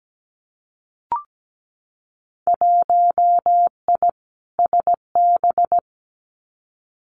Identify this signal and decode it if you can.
Morse code, heard as a keyed tone, about 17 wpm, 700 Hz. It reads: 1I SB
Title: Winter Field Day Exchanges 17wpm